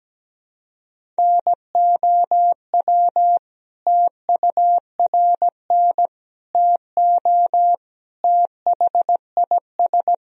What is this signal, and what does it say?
Morse code, heard as a keyed tone, about 17 wpm, 700 Hz. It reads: NOW TURN TO THIS
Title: Sentences from Top 1000 Words - Review Part 1 - 17wpm